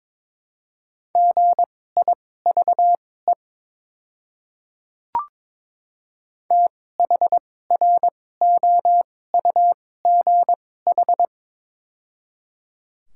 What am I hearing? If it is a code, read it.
Morse code, heard as a keyed tone, about 22 wpm, spaced as for 16 wpm, 700 Hz. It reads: GIVE THROUGH
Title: 2x Character Spacing - Top 200 Words - 22wpm